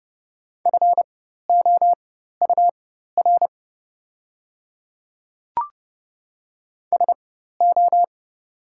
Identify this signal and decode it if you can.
Morse code, heard as a keyed tone, about 29 wpm, spaced as for 14 wpm, 700 Hz. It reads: FOUR SO
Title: 4x Character Spacing - Top 500 Words - 30wpm